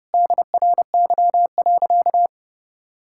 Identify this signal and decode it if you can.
Morse code, heard as a keyed tone, about 32 wpm, 700 Hz. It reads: DRY.
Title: Sentences from Top 1600 Words 30wpm